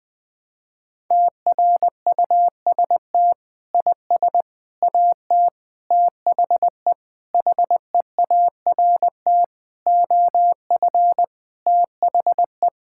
Morse code, heard as a keyed tone, about 21 wpm, 700 Hz. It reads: TRUST IS AT THE HEART OF THE
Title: Sentences with 6-Word Phrases - 20wpm